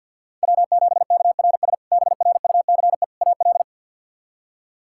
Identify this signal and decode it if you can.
Morse code, heard as a keyed tone, about 52 wpm, 700 Hz. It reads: W7KRS BRUCE AL
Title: K1USN SST QSO Practice - 50wpm